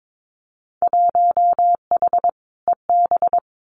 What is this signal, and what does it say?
Morse code, heard as a keyed tone, about 22 wpm, 700 Hz. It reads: 1H EB
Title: Winter Field Day Exchanges Encore 22wpm